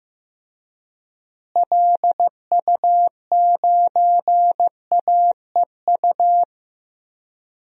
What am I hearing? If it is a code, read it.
Morse code, heard as a keyed tone, about 15 wpm, 700 Hz. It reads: LU9AEU
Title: DXCC-100 Call Signs - 6 Characters - Repeated 2x - 15wpm